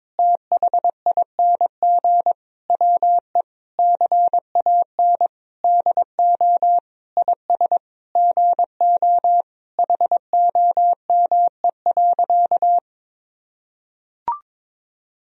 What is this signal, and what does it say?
Morse code, heard as a keyed tone, about 22 wpm, 700 Hz. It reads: THING WE CAN DO IS GO HOME.
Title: Sentences from Top 200 Words and Abbreviations 22wpm